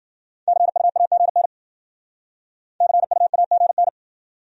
Morse code, heard as a keyed tone, about 50 wpm, 700 Hz. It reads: CUAGN CUAGN
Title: Speed-Racing Abbreviations - 40wpm